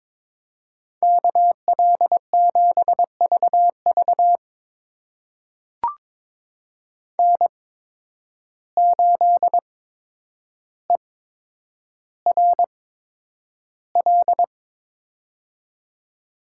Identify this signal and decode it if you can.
Morse code, heard as a keyed tone, about 22 wpm, spaced as for 9 wpm, 700 Hz. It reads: KL7VV N8ERL
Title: USA Call Signs - 5 Characters - 8x Character Spacing - 22wpm